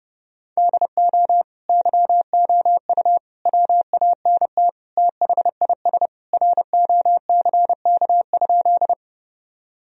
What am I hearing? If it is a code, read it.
Morse code, heard as a keyed tone, about 30 wpm, 700 Hz. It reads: DO YOU WANT THIS ROCK?
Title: Sentences from Top 1000 Words - Review Part 2 - 30wpm